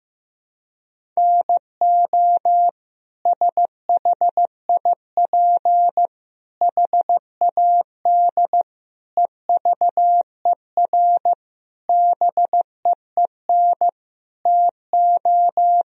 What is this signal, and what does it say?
Morse code, heard as a keyed tone, about 15 wpm, 700 Hz. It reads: NO SHIP HAD EVER BEEN TO
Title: Sentences from Top 1000 Words - Review Part 4 - 15wpm